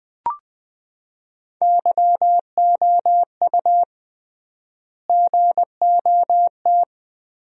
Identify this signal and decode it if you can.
Morse code, heard as a keyed tone, about 20 wpm, 700 Hz. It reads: YOU GOT
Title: 3x Word-Spacing - Sentences from Top 300 Words - 20wpm